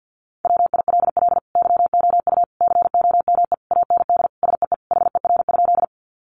Morse code, heard as a keyed tone, about 50 wpm, 700 Hz. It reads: WILL YOU COME AND SEE HER?
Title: Sentences from Top 100 Words 50wpm